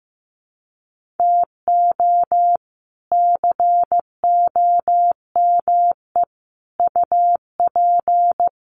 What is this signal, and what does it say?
Morse code, heard as a keyed tone, about 15 wpm, 700 Hz. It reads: TO COME UP
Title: Sets of 5 Words 15wpm